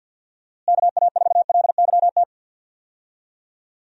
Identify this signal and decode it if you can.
Morse code, heard as a keyed tone, about 52 wpm, 700 Hz. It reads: KA4LYT
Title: USA Call Signs - 6 Characters - Repeated 3x - 50wpm